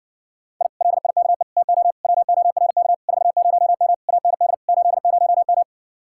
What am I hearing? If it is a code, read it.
Morse code, heard as a keyed tone, about 60 wpm, 700 Hz. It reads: I LIKE TO WORK 40M AND 80M